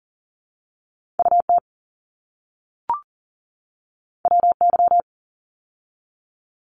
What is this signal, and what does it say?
Morse code, heard as a keyed tone, about 40 wpm, 700 Hz. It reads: UT WY
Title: ARRL Sections 40wpm